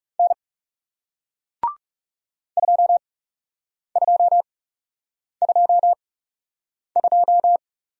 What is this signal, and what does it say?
Morse code, heard as a keyed tone, about 40 wpm, spaced as for 8 wpm, 700 Hz. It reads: N 2222